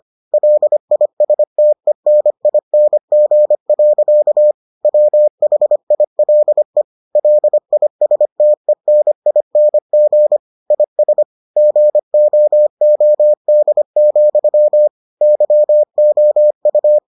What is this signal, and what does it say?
Morse code, heard as a keyed tone, about 25 wpm, 600 Hz. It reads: LISTENING. WHILE LISTENING IS GOOD, YOU